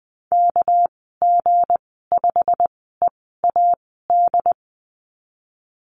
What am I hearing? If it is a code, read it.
Morse code, heard as a keyed tone, about 20 wpm, spaced as for 15 wpm, 700 Hz. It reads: KG5EAD
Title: 219 Lesson 35g - Number 0 - Call Signs 20wpm